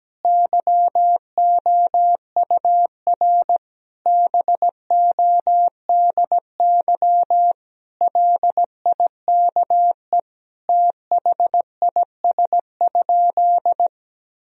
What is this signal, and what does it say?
Morse code, heard as a keyed tone, about 17 wpm, 700 Hz. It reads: YOUR BODY LIKE THIS?
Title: Sentences from Top 1000 Words - Review Part 2 - 17wpm